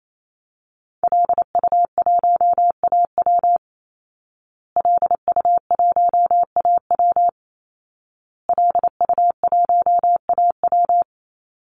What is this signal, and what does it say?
Morse code, heard as a keyed tone, about 28 wpm, 700 Hz. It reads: LU1AW LU1AW LU1AW
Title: Call Signs Repeated 2x - 28wpm